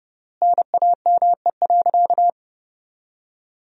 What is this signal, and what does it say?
Morse code, heard as a keyed tone, about 30 wpm, 700 Hz. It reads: NAME.
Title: 3x Word-Spacing - Sentences from Top 200 Words - 30wpm